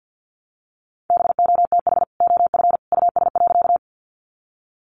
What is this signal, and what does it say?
Morse code, heard as a keyed tone, about 50 wpm, 700 Hz. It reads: BOTH OF US.
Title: Sentences from Top 300 Words 50wpm